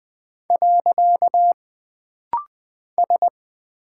Morse code, heard as a keyed tone, about 20 wpm, 700 Hz. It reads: . S
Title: Single Character (Letter:Number:Punctuation:Prosign) - Rapid-Fire - 20wpm